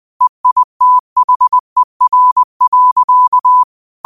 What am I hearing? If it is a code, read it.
Morse code, heard as a keyed tone, about 20 wpm, 1000 Hz. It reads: EITHER.